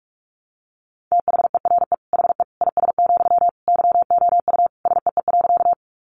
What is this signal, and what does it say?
Morse code, heard as a keyed tone, about 44 wpm, 700 Hz. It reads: THERE HE IS, YOU SEE.
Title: Sentences from Top 100 Words 45wpm